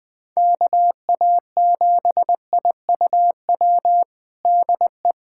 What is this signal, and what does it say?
Morse code, heard as a keyed tone, about 21 wpm, 700 Hz. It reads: KA7IUW DE
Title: USA Callsign DE Callsign - 6 Characters - 20wpm